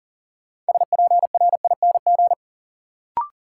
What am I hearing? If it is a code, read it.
Morse code, heard as a keyed tone, about 40 wpm, 700 Hz. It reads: SPRING
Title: Sentences from Top 1200 Words - New Word Drill Alt - 40wpm